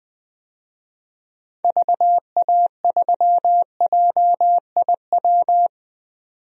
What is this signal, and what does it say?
Morse code, heard as a keyed tone, about 21 wpm, 700 Hz. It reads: VA3JIW
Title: DXCC-100 Call Signs - 6 Characters - 20wpm